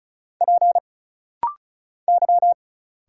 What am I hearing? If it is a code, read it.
Morse code, heard as a keyed tone, about 34 wpm, 700 Hz. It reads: P Y
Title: Single Letters - Rapid-Fire 35wpm